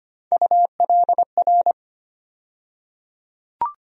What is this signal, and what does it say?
Morse code, heard as a keyed tone, about 25 wpm, 700 Hz. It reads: ULR
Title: USA Call Sign Postfixes - 3 Letters 25wpm